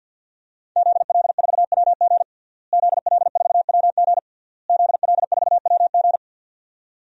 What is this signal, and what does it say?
Morse code, heard as a keyed tone, about 52 wpm, 700 Hz. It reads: ZL4WG ZL4WG ZL4WG
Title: DXCC-100 Call Signs - 5 Characters - Repeated 3x - 50wpm